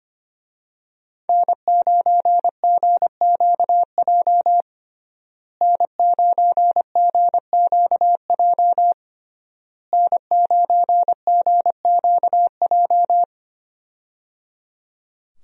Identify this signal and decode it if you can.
Morse code, heard as a keyed tone, about 25 wpm, 700 Hz. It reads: N9GQJ N9GQJ N9GQJ
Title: USA Call Signs - 5 Characters - Repeated 3x - 25wpm